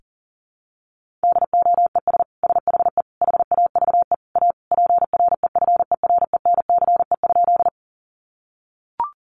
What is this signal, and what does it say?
Morse code, heard as a keyed tone, about 40 wpm, 700 Hz. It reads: DOES SHE HAVE A PREFERENCE?
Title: Sentences from Top 1500 Words 40wpm